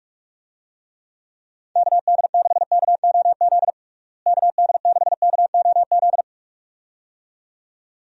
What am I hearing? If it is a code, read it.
Morse code, heard as a keyed tone, about 46 wpm, 700 Hz. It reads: KD6KOZ KD6KOZ
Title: USA Call Signs - 6 Characters - Repeated 2x - 45wpm